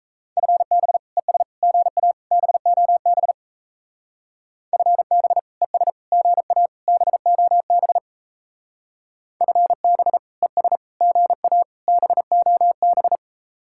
Morse code, heard as a keyed tone, about 34 wpm, 700 Hz. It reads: FB ES GA BOB FB ES GA BOB FB ES GA BOB
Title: Speed-Racing Common QSO Phrases - 28wpm